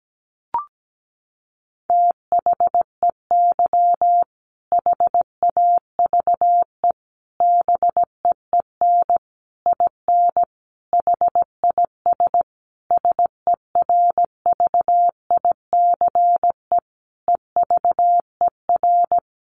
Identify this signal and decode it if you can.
Morse code, heard as a keyed tone, about 17 wpm, 700 Hz. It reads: THEY HAVE BEEN IN HIS SERVICE EVER
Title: Sentences from Top 700 Words 17wpm